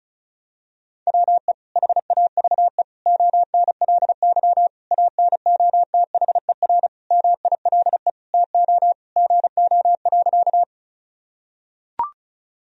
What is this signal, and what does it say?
Morse code, heard as a keyed tone, about 36 wpm, 700 Hz. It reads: WE HAVE ONLY ANOTHER MILE TO GO.
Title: Sentences from Top 1000 Words - Review Part 1 - 35wpm